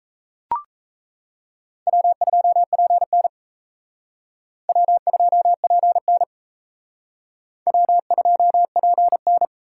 Morse code, heard as a keyed tone, about 35 wpm, 700 Hz. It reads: W2PN W2PN W2PN
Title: Speed Racing - USA 1x2 Call Signs 28wpm